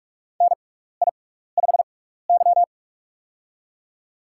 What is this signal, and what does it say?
Morse code, heard as a keyed tone, about 42 wpm, spaced as for 15 wpm, 700 Hz. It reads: NI5Y
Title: USA Call Signs - 4 Characters - 6x Character Spacing - 45wpm